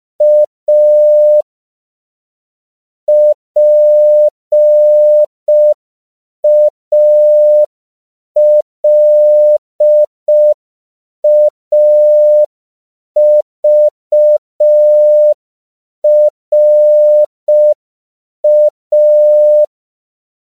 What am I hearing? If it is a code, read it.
Morse code, heard as a keyed tone, about 5 wpm, 600 Hz. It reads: A PALAVRA
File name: CW / 5PPM